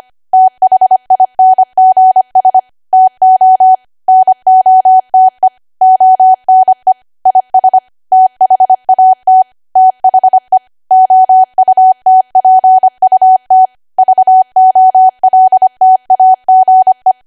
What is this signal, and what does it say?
Morse code, heard as a keyed tone, about 25 wpm, 750 Hz. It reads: THINGS TO NOTE ONE IS THAT THE OUTPUT VOLTAGE
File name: cwa / W1AW-Level3 / 25WPM